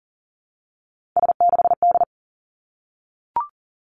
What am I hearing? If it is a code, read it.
Morse code, heard as a keyed tone, about 42 wpm, 700 Hz. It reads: S6D
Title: Three Challenging Letter-Number Combinations 40wpm